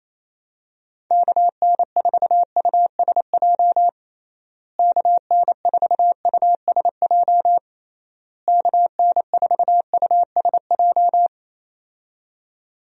Morse code, heard as a keyed tone, about 28 wpm, 700 Hz. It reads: KN4USJ KN4USJ KN4USJ
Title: USA Call Signs - 6 Characters - Repeated 3x - 28wpm